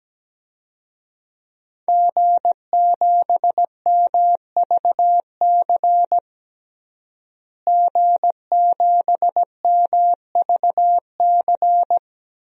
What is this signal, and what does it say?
Morse code, heard as a keyed tone, about 17 wpm, 700 Hz. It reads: G7MVC G7MVC
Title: DXCC-100 Call Signs - 5 Characters - Repeated 3x - 17wpm